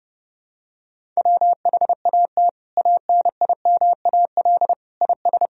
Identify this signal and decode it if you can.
Morse code, heard as a keyed tone, about 32 wpm, 700 Hz. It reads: WHAT ANIMAL IS
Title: Sentences from Top 1000 Words - Review Part 1 - 30wpm